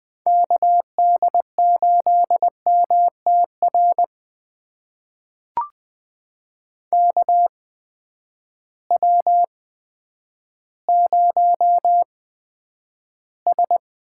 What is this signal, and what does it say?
Morse code, heard as a keyed tone, about 20 wpm, spaced as for 10 wpm, 700 Hz. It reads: KD8MTR KW0S
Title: USA Call Signs - 8x Character Spacing - 20wpm